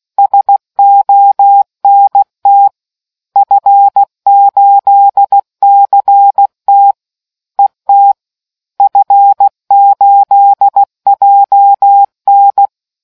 Morse code, heard as a keyed tone, about 16 wpm, 800 Hz. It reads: SONT F8CT ET F8JN